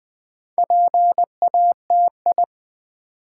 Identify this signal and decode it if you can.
Morse code, heard as a keyed tone, about 20 wpm, 700 Hz. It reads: PATI
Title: Build A Word - 2 Letters At A Time - 8-Letter Words - 20wpm